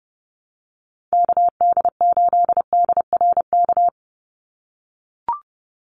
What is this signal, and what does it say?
Morse code, heard as a keyed tone, about 30 wpm, 700 Hz. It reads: KD8DRK
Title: USA Call Signs 30wpm